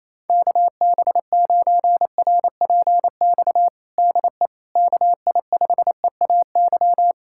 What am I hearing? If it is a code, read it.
Morse code, heard as a keyed tone, about 28 wpm, 700 Hz. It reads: KB9RPX DE KI5EAY
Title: USA Callsign DE Callsign - 6 Characters - 28wpm